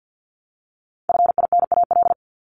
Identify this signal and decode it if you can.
Morse code, heard as a keyed tone, about 50 wpm, 700 Hz. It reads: FINAL
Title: Top 500 Words Repeated 50wpm